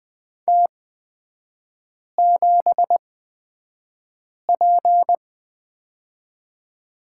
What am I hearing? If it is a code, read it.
Morse code, heard as a keyed tone, about 20 wpm, spaced as for 5 wpm, 700 Hz. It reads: T7P
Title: 243 Lesson 39e - Letter z - Three Characters 20wpm